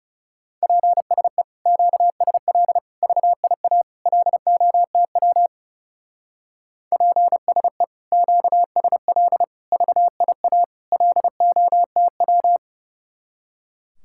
Morse code, heard as a keyed tone, about 36 wpm, 700 Hz. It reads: PSE QSL VIA LOTW PSE QSL VIA LOTW
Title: Speed-Racing Common QSO Phrases - 30wpm